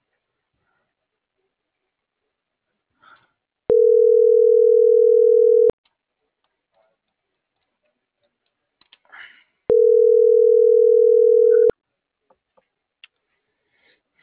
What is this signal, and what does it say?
Ringback tone (North America), 440 + 480 Hz.